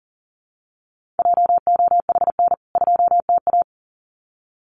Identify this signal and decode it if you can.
Morse code, heard as a keyed tone, about 40 wpm, 700 Hz. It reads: JOHN 2TA
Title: CWT Exchange 40wpm